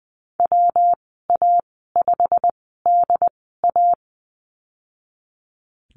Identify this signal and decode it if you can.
Morse code, heard as a keyed tone, about 20 wpm, spaced as for 15 wpm, 700 Hz. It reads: WA5DA
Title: 154 Lesson 25g - Letter p - Call Signs 20wpm